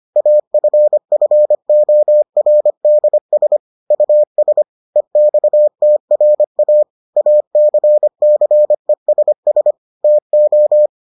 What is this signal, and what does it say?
Morse code, heard as a keyed tone, about 25 wpm, 600 Hz. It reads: AFFORDS US EXTRA ACCESS TO